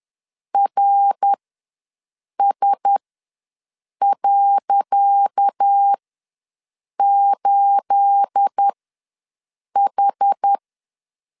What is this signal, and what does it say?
Morse code, heard as a keyed tone, about 10 wpm, spaced as for 6 wpm, 800 Hz. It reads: RS.8H
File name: LEZIONI / 10 SIGNS / 10C DOT ws 10wpm 4213